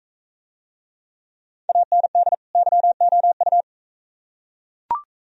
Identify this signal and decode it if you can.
Morse code, heard as a keyed tone, about 41 wpm, 700 Hz. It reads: AND YOU